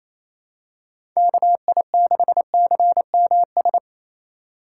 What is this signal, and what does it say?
Morse code, heard as a keyed tone, about 27 wpm, 700 Hz. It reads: KI6CMS
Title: USA Call Signs - 6 Characters - 3x Character Spacing - 28wpm